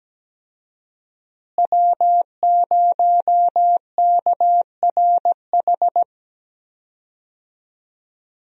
Morse code, heard as a keyed tone, about 17 wpm, 700 Hz. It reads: W0KRH
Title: USA Call Signs - 5 Characters - Repeated 2x - 17wpm